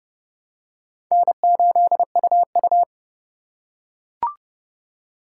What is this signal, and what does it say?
Morse code, heard as a keyed tone, about 30 wpm, 700 Hz. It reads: N8UU
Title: USA Call Signs - 4 Characters - 3x Character Spacing - 30wpm